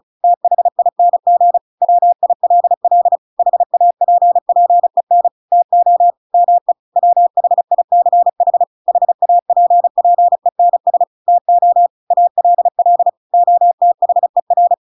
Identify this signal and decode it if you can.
Morse code, heard as a keyed tone, about 36 wpm, 700 Hz. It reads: THING WILL HAPPEN TO ME WHICH HAPPENS TO ALL OTHER